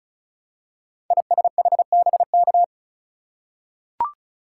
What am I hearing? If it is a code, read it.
Morse code, heard as a keyed tone, about 36 wpm, 700 Hz. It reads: ISHBK
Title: Code Groups - 5 Letters - 35wpm